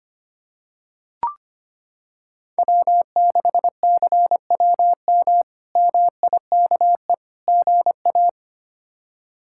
Morse code, heard as a keyed tone, about 25 wpm, 700 Hz. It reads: W6CWM MIKE GA
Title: K1USN SST QSO Practice - 25wpm